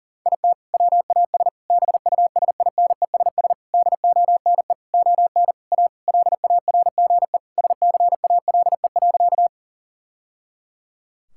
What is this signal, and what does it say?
Morse code, heard as a keyed tone, about 40 wpm, 700 Hz. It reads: IT WAS BUSINESS DONE ON A LARGE SCALE.
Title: Sentences from Top 1000 Words - Review Part 5 - 40wpm